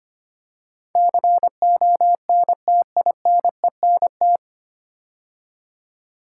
Morse code, heard as a keyed tone, about 25 wpm, 690 Hz. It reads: CONTINENT
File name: practice / 25wpm / 9 Letter Words - First 3 Letters Given - 25wpm